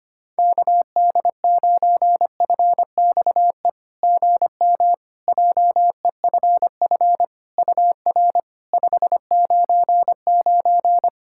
Morse code, heard as a keyed tone, about 25 wpm, 700 Hz. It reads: KD9FXE GM JEFF UR 599
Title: SOTA QSO Practice 25wpm